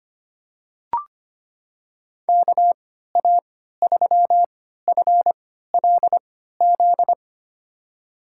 Morse code, heard as a keyed tone, about 25 wpm, spaced as for 14 wpm, 700 Hz. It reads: KA3FLZ
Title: USA Call Signs - 6 Characters - 3x Character Spacing - 25wpm